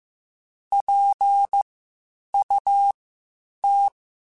Morse code, heard as a keyed tone, about 15 wpm, spaced as for 8 wpm, 800 Hz. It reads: PUT